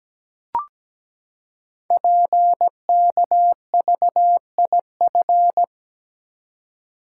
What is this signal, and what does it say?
Morse code, heard as a keyed tone, about 17 wpm, 700 Hz. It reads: PKVIF
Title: Code Groups - 5 Letters - 17wpm